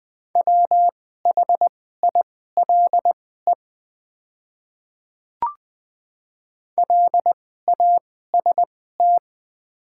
Morse code, heard as a keyed tone, about 20 wpm, spaced as for 14 wpm, 690 Hz. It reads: WHILE LAST
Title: 2x Character Spacing - Top 500 Words - 20wpm